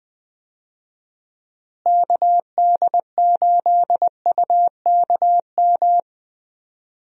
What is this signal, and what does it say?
Morse code, heard as a keyed tone, about 20 wpm, 700 Hz. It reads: KD8UKM
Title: USA Call Signs - 6 Characters - Repeated 2x - 20wpm